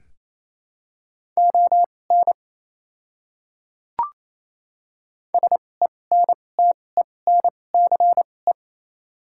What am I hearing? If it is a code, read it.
Morse code, heard as a keyed tone, about 28 wpm, spaced as for 20 wpm, 690 Hz. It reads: ON SENTENCE
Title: 2x Character Spacing - Top 200 Words - 28wpm